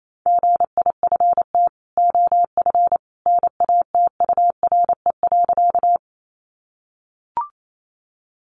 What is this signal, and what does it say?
Morse code, heard as a keyed tone, about 28 wpm, 700 Hz. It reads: GIFT OF NATURE.
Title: Sentences from Top 900 Words 28wpm